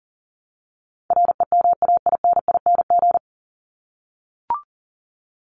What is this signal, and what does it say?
Morse code, heard as a keyed tone, about 40 wpm, 700 Hz. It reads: REMAINING
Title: ING Suffix - Top 100 Words with Base Words 40wpm